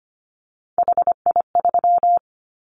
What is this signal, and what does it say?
Morse code, heard as a keyed tone, about 25 wpm, 700 Hz. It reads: HI3
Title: Callsign Prefixes for Top 100 DXCC - 25wpm